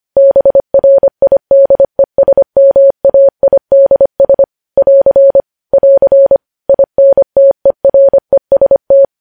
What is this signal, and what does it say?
Morse code, heard as a keyed tone, about 25 wpm, 550 Hz. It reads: BRIDESMAIDS <AR> <AR> INTEREST
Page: news headlines in morse code at 25 wpm